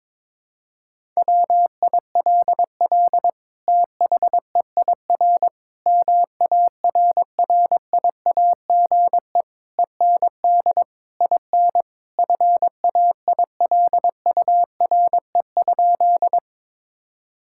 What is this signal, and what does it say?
Morse code, heard as a keyed tone, about 22 wpm, 700 Hz. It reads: WILL THEIR MARRIAGE END IN FAILURE?
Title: Sentences from Top 1000 Words - Review Part 6 - 22wpm